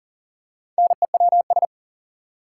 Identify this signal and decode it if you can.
Morse code, heard as a keyed tone, about 40 wpm, 700 Hz. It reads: NEWS
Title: Four Letter Words - 40wpm